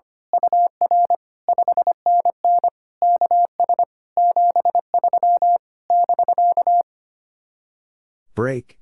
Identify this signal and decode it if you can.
Morse code, heard as a keyed tone, about 24 wpm, 700 Hz. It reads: UR 5NN KS 73 <BK>
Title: Speed-Racing Common QSO Phrases - 25wpm